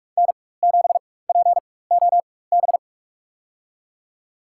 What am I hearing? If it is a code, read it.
Morse code, heard as a keyed tone, about 45 wpm, spaced as for 21 wpm, 700 Hz. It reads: N7POB
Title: USA Call Signs - 4x Character Spacing - 45wpm